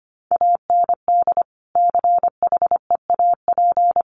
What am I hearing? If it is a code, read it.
Morse code, heard as a keyed tone, about 25 wpm, 700 Hz. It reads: AND CHEAP
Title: Sentences from Top 1200 Words 25wpm